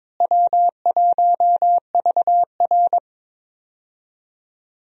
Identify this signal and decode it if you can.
Morse code, heard as a keyed tone, about 21 wpm, 700 Hz. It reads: W1VR
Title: USA Call Signs - 4 Characters - Repeated 2x - 22wpm